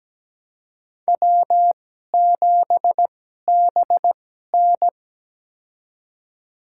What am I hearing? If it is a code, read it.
Morse code, heard as a keyed tone, about 17 wpm, spaced as for 12 wpm, 700 Hz. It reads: W7BN
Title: USA Call Signs - 4 Characters - 2x Character Spacing - 17wpm